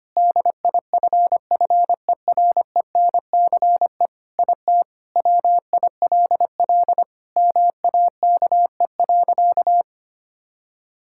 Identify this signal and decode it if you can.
Morse code, heard as a keyed tone, about 25 wpm, 700 Hz. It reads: DIFFERENCE IT WILL MAKE.
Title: Sentences from Top 1000 Words - Review Part 5 - 25wpm